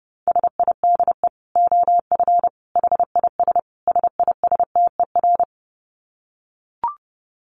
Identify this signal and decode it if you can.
Morse code, heard as a keyed tone, about 30 wpm, 700 Hz. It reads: SIDE OF HIS SISTER
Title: Dit Challenge 30wpm